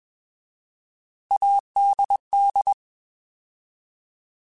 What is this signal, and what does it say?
Morse code, heard as a keyed tone, about 21 wpm, 800 Hz. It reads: ADD
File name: Alphabetical / Alphabetical mp3s / As @20